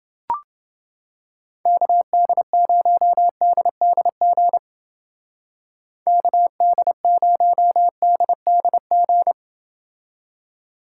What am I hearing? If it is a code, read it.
Morse code, heard as a keyed tone, about 28 wpm, 700 Hz. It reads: KD0DDG KD0DDG